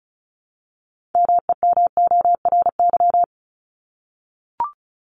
Morse code, heard as a keyed tone, about 34 wpm, 700 Hz. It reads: MEMORY
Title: Top 1251-1500 Words 35wpm